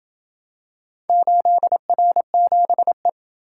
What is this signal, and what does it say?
Morse code, heard as a keyed tone, about 28 wpm, 700 Hz. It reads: 8R7E